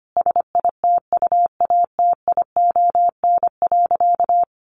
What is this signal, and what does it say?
Morse code, heard as a keyed tone, about 25 wpm, 700 Hz. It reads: SITUATION.
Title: Sentences from Top 1200 Words 25wpm